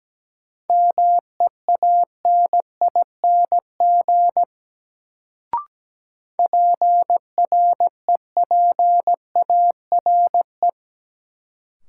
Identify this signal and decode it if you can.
Morse code, heard as a keyed tone, about 17 wpm, 700 Hz. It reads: MEANING PREPARE
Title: Sentences from Top 900 Words - New Word Drill - 17wpm